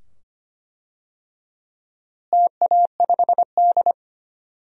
Morse code, heard as a keyed tone, about 25 wpm, 700 Hz. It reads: TA5D
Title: DXCC-100 Call Signs - 4 Characters - Repeated 2x - 25wpm